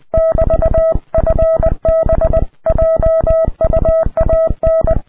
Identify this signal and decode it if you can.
Morse code, heard as a keyed tone, about 20 wpm, 650 Hz. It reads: <BT>FB JUAN
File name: ea5xq / qso ea5bm xe2 ea5xq qrp